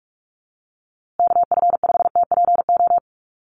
Morse code, heard as a keyed tone, about 44 wpm, 700 Hz. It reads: KF5TPO
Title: USA Call Signs Repeated 2x 45wpm